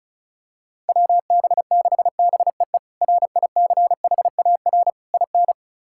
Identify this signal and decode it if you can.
Morse code, heard as a keyed tone, about 36 wpm, 700 Hz. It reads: WB6BEE RICHAR IN
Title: K1USN SST QSO Practice - 35wpm